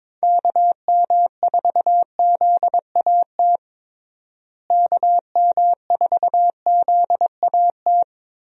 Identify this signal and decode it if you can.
Morse code, heard as a keyed tone, about 22 wpm, 700 Hz. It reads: KM4ZAT KM4ZAT
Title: USA Call Signs - 6 Characters - Repeated 3x - 22wpm